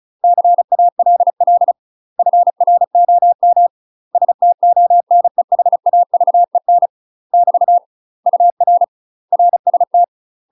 Morse code, heard as a keyed tone, about 34 wpm, 700 Hz. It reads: CALL FROM STONEHAVEN <BT> UR RST
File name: UK MASTER QSOs TRAINING / 35WPM / UK MASTER QSOs 0031